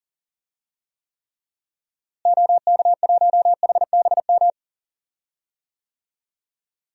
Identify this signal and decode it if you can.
Morse code, heard as a keyed tone, about 40 wpm, 700 Hz. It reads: OK1HBM
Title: DXCC-100 Call Signs - 6 Characters - Repeated 2x - 40wpm